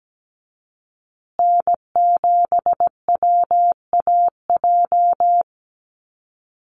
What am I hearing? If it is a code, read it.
Morse code, heard as a keyed tone, about 17 wpm, 700 Hz. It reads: N7WAJ
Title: USA Call Signs Repeated 2x 17wpm